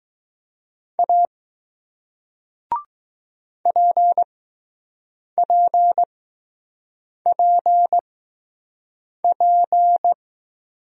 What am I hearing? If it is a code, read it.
Morse code, heard as a keyed tone, about 20 wpm, spaced as for 6 wpm, 700 Hz. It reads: A PPPP